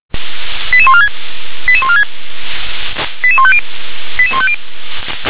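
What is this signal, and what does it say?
Five-tone selective call (ZVEI), heard as quick runs of five tones, tones of about 70 ms. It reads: 88146 88146 88150 88150
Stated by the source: ZVEI five-tone selective call received on an SDR.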